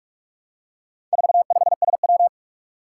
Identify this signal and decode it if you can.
Morse code, heard as a keyed tone, about 46 wpm, 700 Hz. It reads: 45SW
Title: Four Challenging Letter-Number Combinations 45wpm